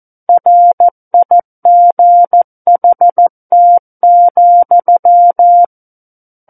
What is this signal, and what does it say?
Morse code, heard as a keyed tone, about 14 wpm, 700 Hz. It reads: RIGHT,
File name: Book One / 05-Chapter 5 - 14 WPM